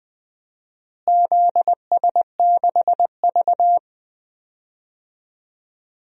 Morse code, heard as a keyed tone, about 21 wpm, 700 Hz. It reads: ZS6V
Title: DXCC-100 Call Signs - 4 Characters - Repeated 3x - 20wpm